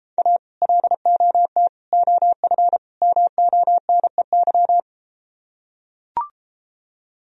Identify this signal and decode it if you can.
Morse code, heard as a keyed tone, about 32 wpm, 700 Hz. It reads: A LOT OF MONEY